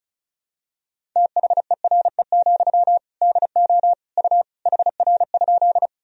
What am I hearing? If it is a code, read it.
Morse code, heard as a keyed tone, about 36 wpm, 700 Hz. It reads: THERE, DO U HR?
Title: Sentences from Top 300 Words and Abbreviations 35wpm